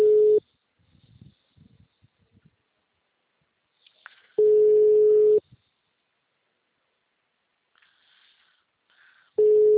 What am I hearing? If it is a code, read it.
Ringback tone (Europe, 425 Hz).
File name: out- -2007- - - .382